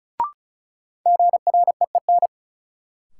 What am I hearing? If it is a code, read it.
Morse code, heard as a keyed tone, about 36 wpm, 700 Hz. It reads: GREEN
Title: Sentences from Top 500 Words - New Word Drill - 35wpm